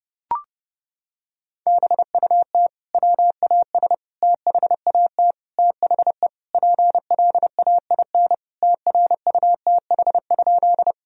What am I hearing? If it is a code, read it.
Morse code, heard as a keyed tone, about 28 wpm, 700 Hz. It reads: BUT WAS THAT THE PLAIN TRUTH?
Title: Sentences from Top 1000 Words - Review Part 2 - 30wpm